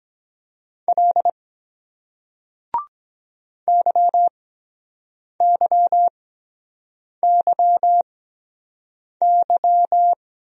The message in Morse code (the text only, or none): L YYYY